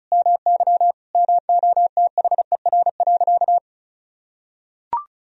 Morse code, heard as a keyed tone, about 34 wpm, 700 Hz. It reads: MY MOTHER.
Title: Sentences from Top 1000 Words - Review Part 4 - 35wpm